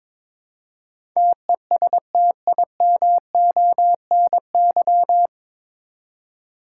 Morse code, heard as a keyed tone, about 22 wpm, 700 Hz. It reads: TESTIMONY
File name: 9 Letter Words - First 3 Letters Given - 22wpm